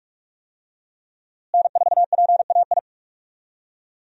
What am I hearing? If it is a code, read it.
Morse code, heard as a keyed tone, about 43 wpm, 700 Hz. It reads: N4PAI
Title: USA Call Signs - 5 Characters - Repeated 2x - 45wpm